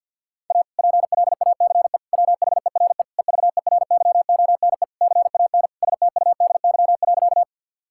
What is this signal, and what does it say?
Morse code, heard as a keyed tone, about 50 wpm, 700 Hz. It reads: A PLACE WHERE EVERYONE CAN STUDY.
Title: Sentences from Top 1000 Words - Review Part 5 - 50wpm